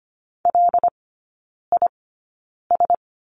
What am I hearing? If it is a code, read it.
Morse code, heard as a keyed tone, about 25 wpm, spaced as for 9 wpm, 690 Hz. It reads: LIS
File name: Lesson 14e - Number 2 - Three Characters 25wpm